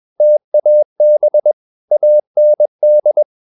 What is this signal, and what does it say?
Morse code, heard as a keyed tone, about 21 wpm, 600 Hz. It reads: TAB AND
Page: Romanen Never af Ken Follet på engelsk.